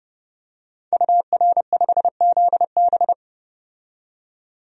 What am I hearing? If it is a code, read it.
Morse code, heard as a keyed tone, about 31 wpm, 700 Hz. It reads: UR5ZB
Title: DXCC-100 Call Signs - 5 Characters - 30wpm